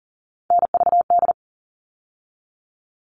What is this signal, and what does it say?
Morse code, heard as a keyed tone, about 40 wpm, 700 Hz. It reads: NVD
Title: All Cut Numbers - 3-Digit 40wpm